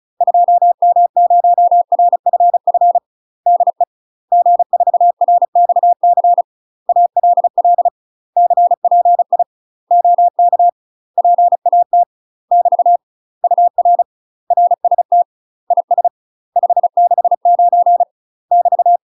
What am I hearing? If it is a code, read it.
Morse code, heard as a keyed tone, about 34 wpm, 700 Hz. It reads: 2M0RFF DE G4RXC ALL CPI OK PAT <BT> UR RST IS 569 <BT>
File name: UK MASTER QSOs TRAINING / 35WPM / UK MASTER QSOs 0043